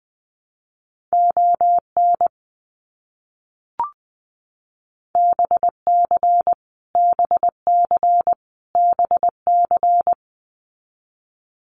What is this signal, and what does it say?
Morse code, heard as a keyed tone, about 19 wpm, 700 Hz. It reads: ON BC BC BC
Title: Canadian Provinces and Territories - Repeated 3x - 20wpm